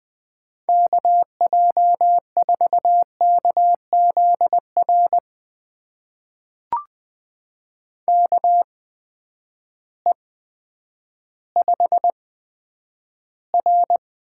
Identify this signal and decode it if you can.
Morse code, heard as a keyed tone, about 20 wpm, spaced as for 10 wpm, 700 Hz. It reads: KJ4KZR KE5R
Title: USA Call Signs - 6 Characters - 8x Character Spacing - 20wpm